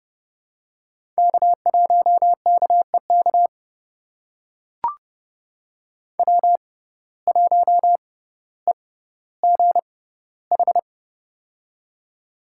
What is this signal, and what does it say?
Morse code, heard as a keyed tone, about 30 wpm, spaced as for 15 wpm, 700 Hz. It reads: K1KEK W1EGH
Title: USA Call Signs - 5 Characters - 6x Character Spacing - 30wpm